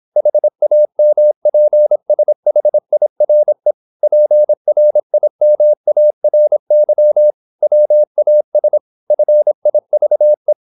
Morse code, heard as a keyed tone, about 26 wpm, 600 Hz. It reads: HAMPSHIRE PRIMARY WAS FIVE
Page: Never 26wpm | CW med Gnister